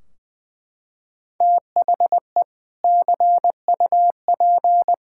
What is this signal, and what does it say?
Morse code, heard as a keyed tone, about 21 wpm, 700 Hz. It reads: THE CUP
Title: Sentences from Top 1600 Words 20wpm